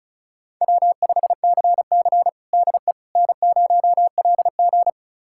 Morse code, heard as a keyed tone, about 34 wpm, 700 Hz. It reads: W5CC DE N0LG
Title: USA Callsign DE Callsign - 4 Characters - 35wpm